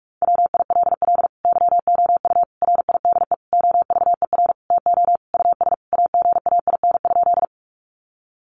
Morse code, heard as a keyed tone, about 44 wpm, 700 Hz. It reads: WILL YOU RIDE OVER TO US AGAIN?
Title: Sentences from Top 1200 Words 45wpm